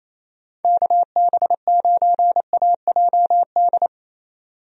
Morse code, heard as a keyed tone, about 27 wpm, 700 Hz. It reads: KB9AJD
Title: USA Call Signs - 6 Characters - 28wpm